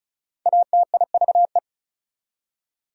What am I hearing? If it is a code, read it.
Morse code, heard as a keyed tone, about 34 wpm, 700 Hz. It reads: ATIVE
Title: Prefixes and Suffixes 35wpm